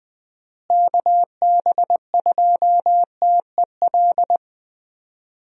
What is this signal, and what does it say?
Morse code, heard as a keyed tone, about 20 wpm, 700 Hz. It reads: KB2TEL
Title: USA Call Signs - 6 Characters - Repeated 3x - 20wpm